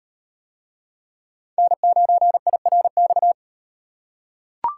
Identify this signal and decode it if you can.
Morse code, heard as a keyed tone, about 38 wpm, 700 Hz. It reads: N9IRX